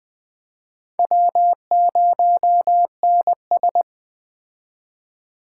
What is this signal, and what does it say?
Morse code, heard as a keyed tone, about 20 wpm, 700 Hz. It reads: W0NS
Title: Speed Racing - USA 1x2 Call Signs 17wpm